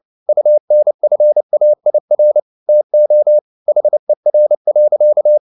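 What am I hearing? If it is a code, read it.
Morse code, heard as a keyed tone, about 29 wpm, 600 Hz. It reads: UNFAIR TO HER.
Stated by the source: Morse code at 29 wpm, sending Ken Follett's novel Never in English, from CW med Gnister.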